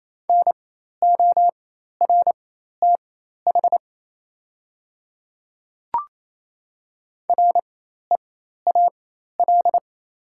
Morse code, heard as a keyed tone, about 28 wpm, spaced as for 13 wpm, 690 Hz. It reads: NORTH REAL
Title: 4x Character Spacing - Top 300 Words - 28wpm